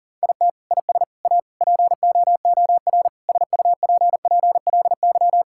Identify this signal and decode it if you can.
Morse code, heard as a keyed tone, about 40 wpm, 700 Hz. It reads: IT IS A POOR SUPPLY
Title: Sentences from Top 1000 Words - Review Part 5 - 40wpm